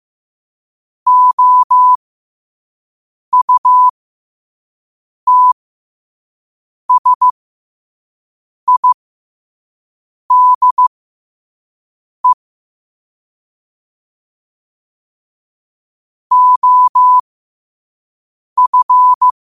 Morse code, heard as a keyed tone, about 15 wpm, spaced as for 5 wpm, 1000 Hz. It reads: OUTSIDE OF
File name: qotdfiles / 5wpm